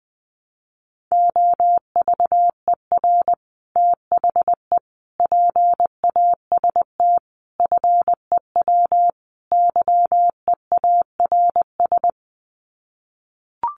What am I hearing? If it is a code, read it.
Morse code, heard as a keyed tone, about 20 wpm, 700 Hz. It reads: OVER THE PAST FEW YEARS
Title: Sets of 5 Words Encore 20wpm